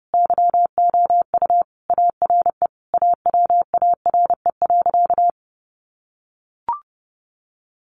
Morse code, heard as a keyed tone, about 30 wpm, 700 Hz. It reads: YOU ARE AWARE.
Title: Sentences from Top 1300 Words 30wpm